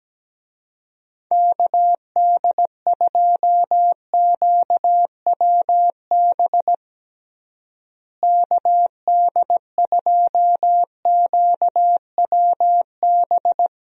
Morse code, heard as a keyed tone, about 17 wpm, 700 Hz. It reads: KD2QWB KD2QWB
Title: USA Call Signs - 6 Characters - Repeated 3x - 17wpm